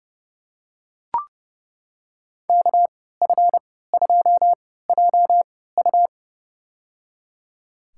Morse code, heard as a keyed tone, about 30 wpm, spaced as for 17 wpm, 700 Hz. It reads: KF2JU
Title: USA Call Signs - 5 Characters - 3x Character Spacing - 30wpm